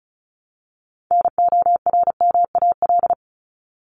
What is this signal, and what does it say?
Morse code, heard as a keyed tone, about 34 wpm, 700 Hz. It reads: NORMAL
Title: Top 1001-1250 Words 35wpm